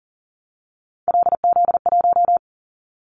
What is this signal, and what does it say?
Morse code, heard as a keyed tone, about 41 wpm, 700 Hz. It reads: LZ1
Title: Callsign Prefixes for Top 50 DXCC - 40wpm